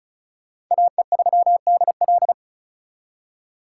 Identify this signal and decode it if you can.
Morse code, heard as a keyed tone, about 45 wpm, 700 Hz. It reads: AE3DL